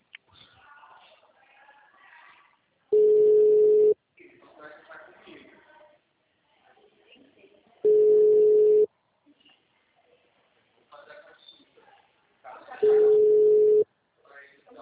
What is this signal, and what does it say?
Ringback tone (Europe, 425 Hz), with speech.
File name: out- -2006- - - .28